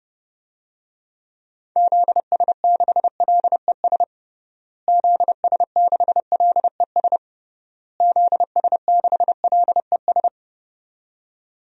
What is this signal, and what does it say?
Morse code, heard as a keyed tone, about 32 wpm, 700 Hz. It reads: ZS6LES ZS6LES ZS6LES
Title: DXCC-100 Call Signs - 6 Characters - Repeated 3x - 30wpm